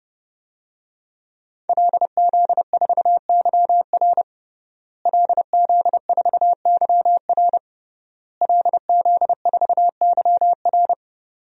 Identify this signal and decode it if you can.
Morse code, heard as a keyed tone, about 30 wpm, 700 Hz. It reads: LZ4YR LZ4YR LZ4YR
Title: DXCC-100 Call Signs - 5 Characters - Repeated 3x - 30wpm